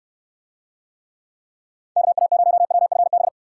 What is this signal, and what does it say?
Morse code, heard as a keyed tone, about 67 wpm, 700 Hz. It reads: KA0WFB